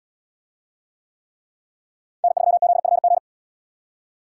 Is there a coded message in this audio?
Yes: Morse code from 2.2 to 3.2 s, heard as a keyed tone, about 77 wpm, 700 Hz. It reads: N3ZFG